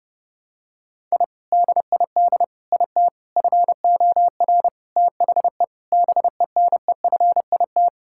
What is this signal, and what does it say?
Morse code, heard as a keyed tone, about 30 wpm, 700 Hz. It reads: I DID IT FOR THE BENEFIT
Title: Sentences from Top 1000 Words - Review Part 6 - 30wpm